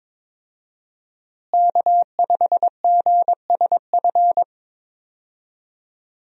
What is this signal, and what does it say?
Morse code, heard as a keyed tone, about 22 wpm, 700 Hz. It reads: K5GSF
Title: USA Call Signs - 5 Characters - Repeated 3x - 22wpm